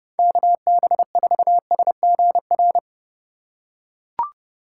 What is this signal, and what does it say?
Morse code, heard as a keyed tone, about 30 wpm, 700 Hz. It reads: KB4SGR
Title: USA Call Signs - 6 Characters - 4x Character Spacing - 30wpm